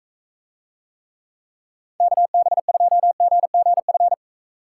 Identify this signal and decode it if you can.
Morse code, heard as a keyed tone, about 45 wpm, 700 Hz. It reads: KB2GGF